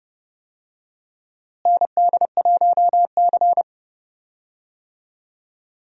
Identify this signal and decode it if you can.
Morse code, heard as a keyed tone, about 32 wpm, 700 Hz. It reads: ND1C
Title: USA Call Signs - 4 Characters - Repeated 3x - 30wpm